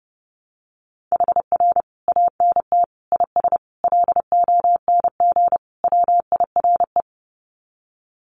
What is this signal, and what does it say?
Morse code, heard as a keyed tone, about 30 wpm, 700 Hz. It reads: HR ANT IS LONG WIRE
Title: Common QSO Phrases 30wpm